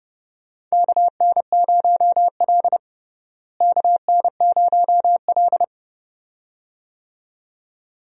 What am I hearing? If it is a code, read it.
Morse code, heard as a keyed tone, about 30 wpm, 700 Hz. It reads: KN0L KN0L
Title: USA Call Signs - 4 Characters - Repeated 2x - 30wpm